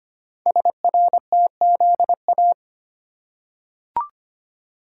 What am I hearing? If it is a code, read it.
Morse code, heard as a keyed tone, about 25 wpm, 690 Hz. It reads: SRTZA
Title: Code Groups - 5 Letters - 25wpm